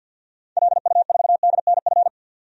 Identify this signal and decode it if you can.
Morse code, heard as a keyed tone, about 53 wpm, 700 Hz. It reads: LU4DNF
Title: DXCC-100 Call Signs - 6 Characters - 50wpm